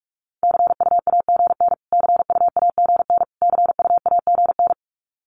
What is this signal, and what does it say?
Morse code, heard as a keyed tone, about 46 wpm, 700 Hz. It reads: CUAGN CUAGN CUAGN
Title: Abbreviations Repeated 3x - 45wpm